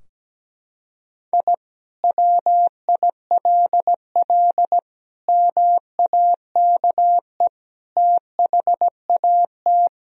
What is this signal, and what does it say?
Morse code, heard as a keyed tone, about 17 wpm, 700 Hz. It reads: I WILL MAKE THAT
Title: Sentences from Top 1000 Words - Review Part 3 - 17wpm